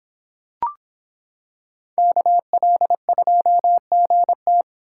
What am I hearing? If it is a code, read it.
Morse code, heard as a keyed tone, about 25 wpm, 700 Hz. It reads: KL2GT